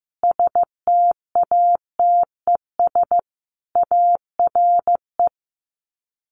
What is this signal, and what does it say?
Morse code, heard as a keyed tone, about 15 wpm, 700 Hz. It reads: STATES ARE
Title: Sets of 5 Words Encore 15wpm